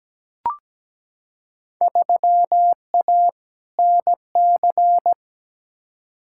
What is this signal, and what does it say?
Morse code, heard as a keyed tone, about 17 wpm, 700 Hz. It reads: 3A NC
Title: Field Day Exchanges - 17wpm